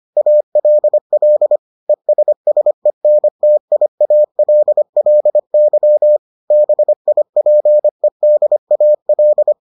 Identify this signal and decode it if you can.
Morse code, heard as a keyed tone, about 25 wpm, 600 Hz. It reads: ALL ESSENTIALLY BIPEDAL